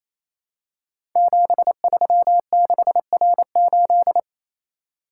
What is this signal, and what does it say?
Morse code, heard as a keyed tone, about 27 wpm, 700 Hz. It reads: 736R8
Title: Code Groups - 5 Letter:Numbers - 28wpm